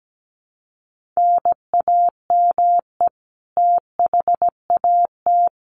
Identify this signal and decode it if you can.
Morse code, heard as a keyed tone, about 17 wpm, 700 Hz. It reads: NAME THAT
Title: Sentences from Top 1300 Words 17wpm